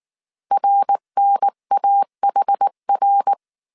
Morse code, heard as a keyed tone, about 19 wpm, 800 Hz. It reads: RNAHR
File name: CW / DET / 1 3 1500C S B 18wpm 1916